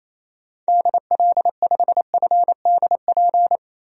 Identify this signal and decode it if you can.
Morse code, heard as a keyed tone, about 28 wpm, 700 Hz. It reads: DL5FDP
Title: DXCC-100 Call Signs - 6 Characters - 28wpm